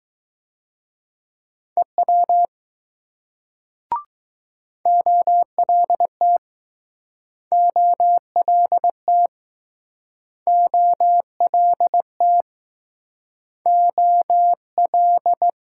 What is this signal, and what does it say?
Morse code, heard as a keyed tone, about 20 wpm, 700 Hz. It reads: EW OLT OLT OLT O L